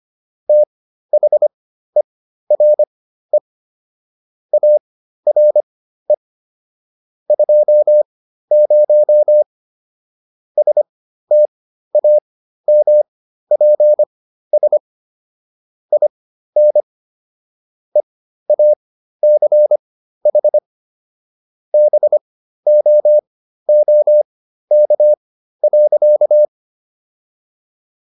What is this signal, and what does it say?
Morse code, heard as a keyed tone, about 25 wpm, spaced as for 13 wpm, 600 Hz. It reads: THERE ARE 20 STAMPS IN EACH BOOK.